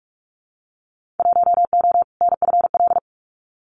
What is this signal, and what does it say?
Morse code, heard as a keyed tone, about 46 wpm, 700 Hz. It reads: 1O NFL
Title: Winter Field Day Exchanges 45wpm